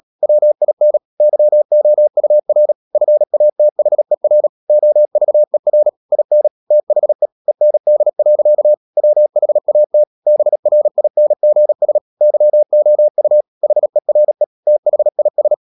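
Morse code, heard as a keyed tone, about 37 wpm, 600 Hz. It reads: WIN YOUR FATHER OVER IN THE END. WHAT BRINGS YOU HERE THIS
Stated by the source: Morse practice at 36 wpm, text of Ken Follett's novel Never, from CW med Gnister.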